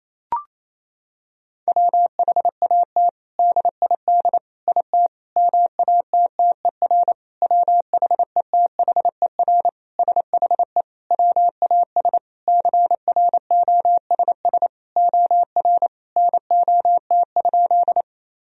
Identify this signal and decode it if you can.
Morse code, heard as a keyed tone, about 28 wpm, 700 Hz. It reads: WHAT DID IT MATTER WHETHER SHE WAS CROSS OR NOT?
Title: Sentences from Top 1000 Words - Review Part 3 - 28wpm